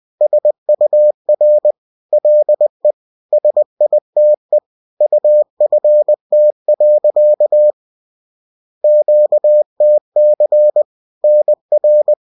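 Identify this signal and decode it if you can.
Morse code, heard as a keyed tone, about 20 wpm, 600 Hz. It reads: SUR LE SITE UFT. QTC NR
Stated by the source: Bulletin sent in CW.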